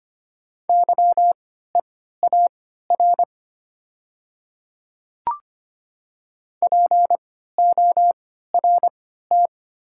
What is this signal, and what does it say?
Morse code, heard as a keyed tone, about 25 wpm, spaced as for 14 wpm, 700 Hz. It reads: YEAR PORT
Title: 3x Character Spacing - Top 200 Words - 25wpm